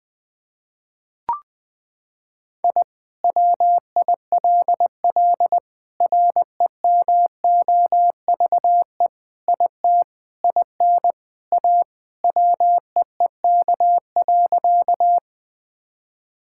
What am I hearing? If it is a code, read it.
Morse code, heard as a keyed tone, about 19 wpm, 700 Hz. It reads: I WILL REMOVE IT IN A WEEK.
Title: Sentences from Top 1000 Words - Review Part 5 - 20wpm